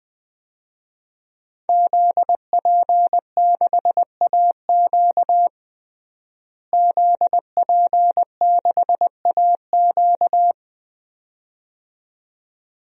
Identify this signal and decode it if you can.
Morse code, heard as a keyed tone, about 20 wpm, 700 Hz. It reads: ZP6AQ ZP6AQ
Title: DXCC-100 Call Signs - 5 Characters - Repeated 2x - 20wpm